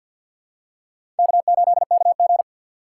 Morse code, heard as a keyed tone, about 52 wpm, 700 Hz. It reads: K8KG
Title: USA Call Signs - 4 Characters - 2x Character Spacing - 50wpm